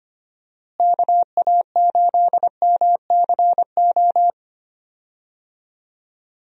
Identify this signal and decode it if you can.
Morse code, heard as a keyed tone, about 25 wpm, 700 Hz. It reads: KA8MCO
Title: USA Call Signs - 6 Characters - Repeated 3x - 25wpm